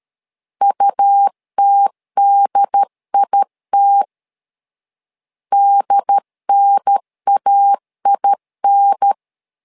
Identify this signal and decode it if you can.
Morse code, heard as a keyed tone, about 13 wpm, 800 Hz. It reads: UTDIT DNAIN
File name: LEZIONI / 2 UHDN / 2 UHDN 12wpm 1456